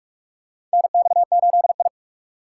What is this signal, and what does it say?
Morse code, heard as a keyed tone, about 44 wpm, 700 Hz. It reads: NX8I
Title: USA Call Signs - 4 Characters - 4x Character Spacing - 45wpm